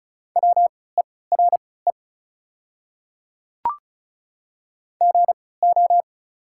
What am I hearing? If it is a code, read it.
Morse code, heard as a keyed tone, about 34 wpm, spaced as for 20 wpm, 690 Hz. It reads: WERE GO
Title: 3x Character Spacing - Top 100 Words - 35wpm